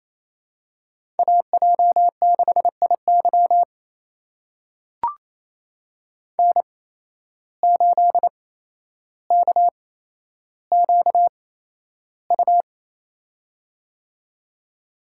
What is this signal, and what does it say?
Morse code, heard as a keyed tone, about 29 wpm, spaced as for 12 wpm, 700 Hz. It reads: AJ6IY N8KQU
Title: USA Call Signs - 5 Characters - 8x Character Spacing - 28wpm